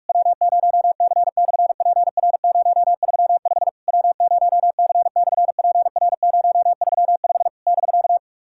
Morse code, heard as a keyed tone, about 44 wpm, 690 Hz. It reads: W0C/PR035 W0C/PR035 <BK>
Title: SOTA QSO Practice 45wpm